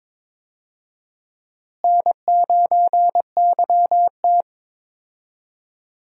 Morse code, heard as a keyed tone, about 22 wpm, 700 Hz. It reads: N9YT
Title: USA Call Signs - 4 Characters - Repeated 2x - 22wpm